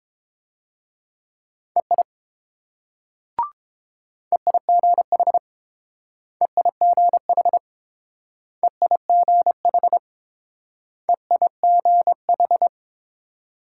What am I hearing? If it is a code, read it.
Morse code, heard as a keyed tone, about 28 wpm, 700 Hz. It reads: EI EIGH EIGH EIGH EIGH